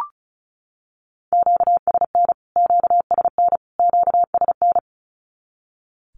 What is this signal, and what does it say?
Morse code, heard as a keyed tone, about 34 wpm, 700 Hz. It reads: QSN QSN QSN
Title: Q-Codes Repeated 3x 35wpm